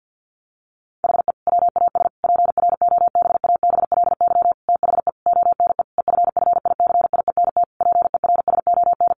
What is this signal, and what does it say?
Morse code, heard as a keyed tone, about 50 wpm, 700 Hz. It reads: HE WAS PROBABLY THE ONE EFFICIENT PERSON
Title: Sentences from Top 1300 Words 50wpm